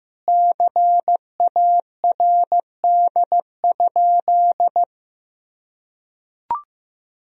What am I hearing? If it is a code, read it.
Morse code, heard as a keyed tone, about 15 wpm, 700 Hz. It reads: CARD?
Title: Sentences from Top 1000 Words - Review Part 4 - 15wpm